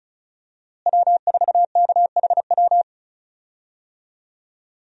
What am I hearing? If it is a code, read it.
Morse code, heard as a keyed tone, about 36 wpm, 700 Hz. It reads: W4KHW
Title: USA Call Signs - 5 Characters - Repeated 2x - 35wpm